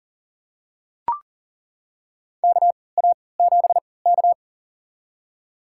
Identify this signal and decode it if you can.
Morse code, heard as a keyed tone, about 38 wpm, spaced as for 22 wpm, 700 Hz. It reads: KA7K
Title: USA Call Signs - 3x Character Spacing - 40wpm